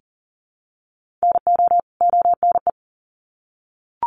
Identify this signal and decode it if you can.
Morse code, heard as a keyed tone, about 40 wpm, 700 Hz. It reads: NO ONE
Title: Sets of 2 Words 40wpm